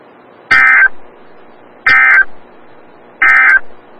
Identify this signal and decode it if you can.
Emergency Alert System (SAME) tones, heard as harsh buzzing bursts. It is heard from 0.4 to 3.7 s.
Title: Des Moines, IA NOAA Weather Radio - WXL57